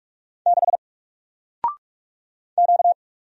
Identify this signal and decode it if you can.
Morse code, heard as a keyed tone, about 48 wpm, 700 Hz. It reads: 6 Q
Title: Single Letter-Number Alt 45wpm